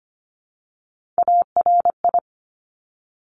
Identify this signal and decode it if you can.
Morse code, heard as a keyed tone, about 25 wpm, 700 Hz. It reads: ARI
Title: Top Three Letter Combinations in Words 25wpm